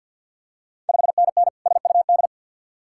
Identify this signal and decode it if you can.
Morse code, heard as a keyed tone, about 51 wpm, 700 Hz. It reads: 5NN SUD
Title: Ontario QSO Party Exchange 50wpm